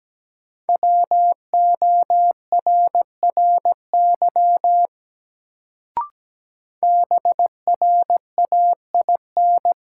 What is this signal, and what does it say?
Morse code, heard as a keyed tone, about 17 wpm, 700 Hz. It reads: WORRY BRAIN
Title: Sentences from Top 1100 Words - New Word Drill Alt - 17wpm